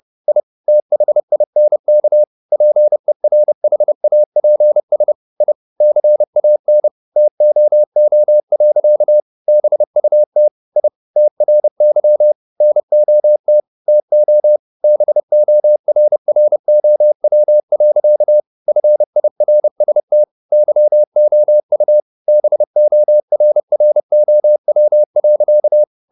Morse code, heard as a keyed tone, about 30 wpm, 600 Hz. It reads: I THINK PERHAPS I CAN TOO. BUT I TRY NOT TO BORROW. FIRST YOU BORROW.
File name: the old man and the sea / the old man and the sea 1 30wpm